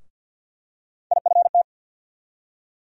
Morse code, heard as a keyed tone, about 48 wpm, 700 Hz. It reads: IFT
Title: USA Call Sign Postfixes - 3 Letters 50wpm